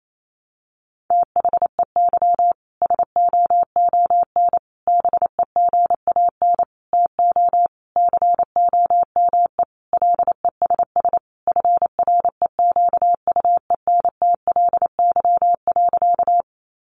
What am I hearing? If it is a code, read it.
Morse code, heard as a keyed tone, about 28 wpm, 700 Hz. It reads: THEY SOON BEGAN TO COME LESS FREQUENTLY.
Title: Sentences from Top 1300 Words 28wpm